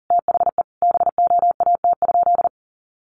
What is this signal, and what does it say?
Morse code, heard as a keyed tone, about 40 wpm, 700 Hz. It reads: THE BOAT?
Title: Sentences from Top 1300 Words 40wpm